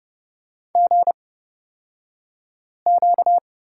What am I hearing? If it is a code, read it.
Morse code, heard as a keyed tone, about 30 wpm, spaced as for 5 wpm, 700 Hz. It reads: GQ
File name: Lesson 38d - Letter q - Two Characters 30wpm